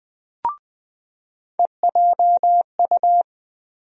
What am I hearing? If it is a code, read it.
Morse code, heard as a keyed tone, about 21 wpm, 700 Hz. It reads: EJU
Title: USA Call Sign Postfixes - 3 Letters 20wpm